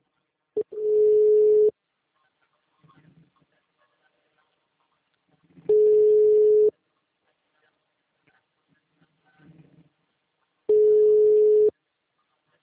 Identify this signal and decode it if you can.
Ringback tone (Europe, 425 Hz).